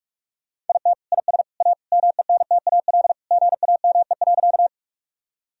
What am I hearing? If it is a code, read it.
Morse code, heard as a keyed tone, about 46 wpm, 700 Hz. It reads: IT IS A MENTAL GAME.
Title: Sentences from Top 1000 Words - Review Part 4 - 45wpm